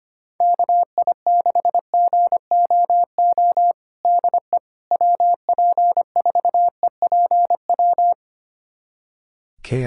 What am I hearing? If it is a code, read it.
Morse code, heard as a keyed tone, about 25 wpm, 700 Hz. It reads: KI6GOO DE WP4EPW
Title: USA Callsign DE Callsign - 6 Characters - 25wpm